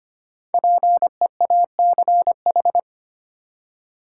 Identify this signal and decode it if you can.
Morse code, heard as a keyed tone, about 25 wpm, 700 Hz. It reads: PEACH
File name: practice / 25wpm / Difficult Words with Hints - 25wpm